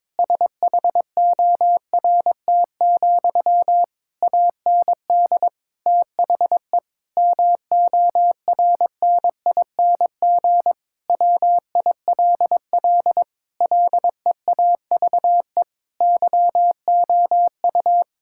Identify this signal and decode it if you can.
Morse code, heard as a keyed tone, about 22 wpm, 700 Hz. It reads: SHORT, AND THE MORNING WILL LEAVE YOU
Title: Sentences from Top 1600 Words 22wpm